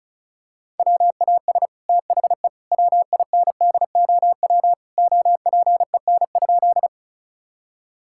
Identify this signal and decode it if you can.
Morse code, heard as a keyed tone, about 36 wpm, 700 Hz. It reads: WAS THE WINDOW OPEN?
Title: Sentences from Top 1600 Words 35wpm